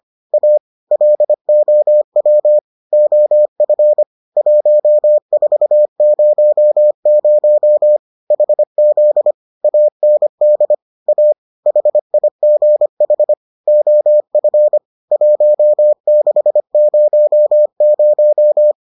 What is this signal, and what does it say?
Morse code, heard as a keyed tone, about 25 wpm, 600 Hz. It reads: A LOW OF 1400 HZ AND A HIGH OF 1600